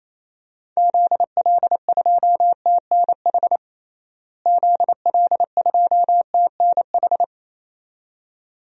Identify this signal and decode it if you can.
Morse code, heard as a keyed tone, about 28 wpm, 700 Hz. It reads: ZL2TNH ZL2TNH
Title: DXCC-100 Call Signs - 6 Characters - Repeated 2x - 28wpm